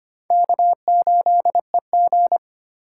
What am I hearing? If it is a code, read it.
Morse code, heard as a keyed tone, about 25 wpm, 700 Hz. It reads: K8EG
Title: USA Call Signs - 4 Characters - 4x Character Spacing - 25wpm